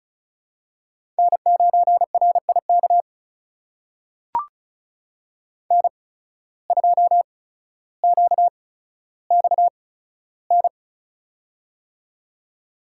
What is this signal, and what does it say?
Morse code, heard as a keyed tone, about 36 wpm, spaced as for 15 wpm, 700 Hz. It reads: N9RIK N2QXN
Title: USA Call Signs - 5 Characters - 8x Character Spacing - 35wpm